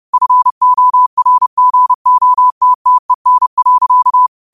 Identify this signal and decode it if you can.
Morse code, heard as a keyed tone, about 30 wpm, 1000 Hz. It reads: FORGOTTEN.